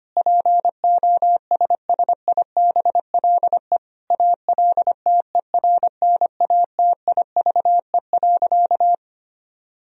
Morse code, heard as a keyed tone, about 25 wpm, 700 Hz. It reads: POSSIBLE ALTERNATIVE.
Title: Sentences from Top 1000 Words - Review Part 4 - 25wpm